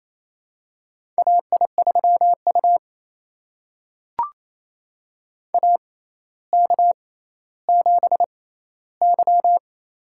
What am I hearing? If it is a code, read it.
Morse code, heard as a keyed tone, about 28 wpm, spaced as for 14 wpm, 700 Hz. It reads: AI3U AK7Y
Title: USA Call Signs - 4 Characters - 6x Character Spacing - 28wpm